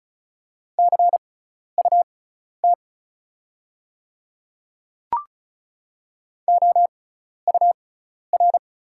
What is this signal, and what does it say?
Morse code, heard as a keyed tone, about 36 wpm, spaced as for 12 wpm, 700 Hz. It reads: CUT OUR
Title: 6x Character Spacing - Top 500 Words - 35wpm